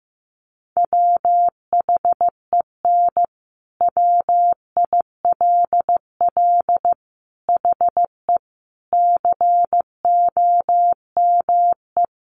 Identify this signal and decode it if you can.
Morse code, heard as a keyed tone, about 15 wpm, 700 Hz. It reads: WHEN WILL HE COME
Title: Sentences from Top 200 Words 15wpm